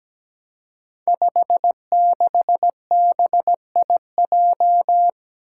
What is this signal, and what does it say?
Morse code, heard as a keyed tone, about 17 wpm, 700 Hz. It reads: 56BIJ
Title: Five Challenging Letter-Number Combinations 17wpm